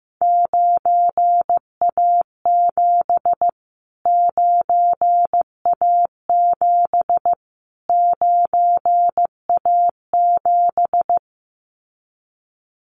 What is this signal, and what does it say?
Morse code, heard as a keyed tone, about 14 wpm, 700 Hz. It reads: 9A7 9A7 9A7
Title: Callsign Prefixes for Top 25 DXCC - Repeated 3x - 15wpm